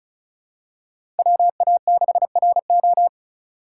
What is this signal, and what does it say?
Morse code, heard as a keyed tone, about 34 wpm, 700 Hz. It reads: WA6RO
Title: USA Call Signs - 3x Character Spacing - 35wpm